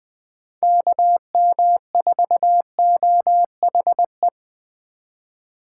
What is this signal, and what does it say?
Morse code, heard as a keyed tone, about 20 wpm, 700 Hz. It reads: KM4OHE